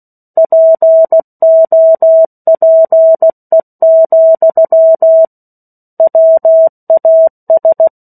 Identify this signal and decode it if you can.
Morse code, heard as a keyed tone, about 16 wpm, 650 Hz. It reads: POPE, WAS
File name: 03-Chapter-03 16 WPM